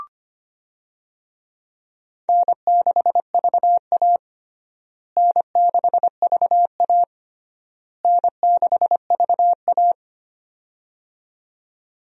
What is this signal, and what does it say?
Morse code, heard as a keyed tone, about 25 wpm, 700 Hz. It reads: N6VA N6VA N6VA
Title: USA Call Signs - 4 Characters - Repeated 3x - 25wpm